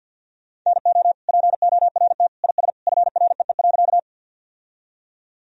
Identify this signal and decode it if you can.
Morse code, heard as a keyed tone, about 50 wpm, 700 Hz. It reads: NO PORT IS FREE.
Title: Sentences from Top 1000 Words - Review Part 1 - 50wpm